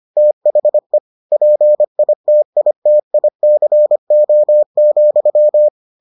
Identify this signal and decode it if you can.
Morse code, heard as a keyed tone, about 25 wpm, 600 Hz. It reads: THE PITITICO,